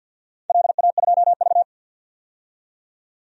Morse code, heard as a keyed tone, about 52 wpm, 700 Hz. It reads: LA2V
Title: DXCC-100 Call Signs - 4 Characters - 50wpm